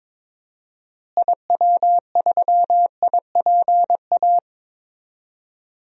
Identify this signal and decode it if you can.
Morse code, heard as a keyed tone, about 22 wpm, 700 Hz. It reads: IW3IPA
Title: DXCC-100 Call Signs - 6 Characters - Repeated 3x - 22wpm